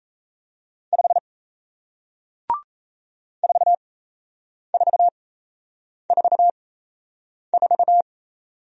Morse code, heard as a keyed tone, about 38 wpm, spaced as for 8 wpm, 700 Hz. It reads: 5 4444